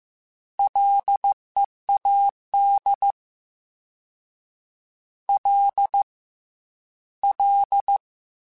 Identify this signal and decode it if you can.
Morse code, heard as a keyed tone, about 15 wpm, 800 Hz. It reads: LEAD L L
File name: Ls / Single L m3s / Single L Prefixe 3x2 Twice @13